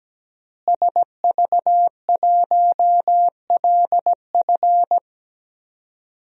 Morse code, heard as a keyed tone, about 17 wpm, 700 Hz. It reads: SV1LF
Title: DXCC-100 Call Signs - 5 Characters - Repeated 3x - 17wpm